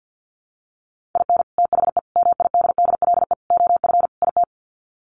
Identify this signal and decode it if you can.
Morse code, heard as a keyed tone, about 50 wpm, 700 Hz. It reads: IN THE MIDDLE OF IT
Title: Sets of 5 Words Encore 50wpm